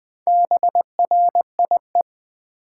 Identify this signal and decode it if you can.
Morse code, heard as a keyed tone, about 20 wpm, 700 Hz. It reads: BRIE
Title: Build A Word - 1 Letter At A Time - 5-Letter Words - 20wpm